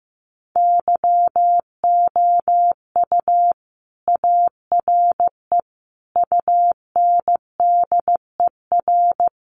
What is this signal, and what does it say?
Morse code, heard as a keyed tone, about 15 wpm, 700 Hz. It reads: YOU ARE UNDER
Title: Sentences from Top 1200 Words 15wpm